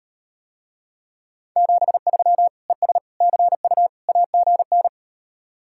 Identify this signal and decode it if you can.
Morse code, heard as a keyed tone, about 38 wpm, 700 Hz. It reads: 73 ES CU AGN